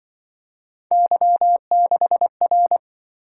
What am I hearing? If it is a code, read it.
Morse code, heard as a keyed tone, about 26 wpm, 700 Hz. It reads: Y6R